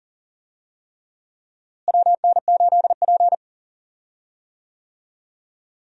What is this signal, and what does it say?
Morse code, heard as a keyed tone, about 40 wpm, 700 Hz. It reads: WN8P
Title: USA Call Signs - 4 Characters - Repeated 3x - 40wpm